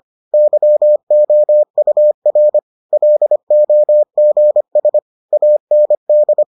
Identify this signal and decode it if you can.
Morse code, heard as a keyed tone, about 25 wpm, 600 Hz. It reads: YOUR LOGS AND